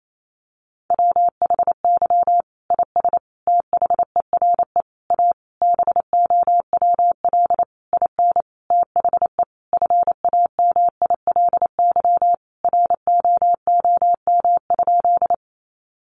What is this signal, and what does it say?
Morse code, heard as a keyed tone, about 28 wpm, 700 Hz. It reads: WHY IS THERE A BOWL IN THE FAMILY ROOM?
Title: Sentences from Top 1200 Words 28wpm